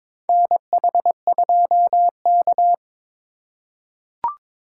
Morse code, heard as a keyed tone, about 22 wpm, 700 Hz. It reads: NH2K
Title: USA Call Signs - 4 Characters - 2x Character Spacing - 22wpm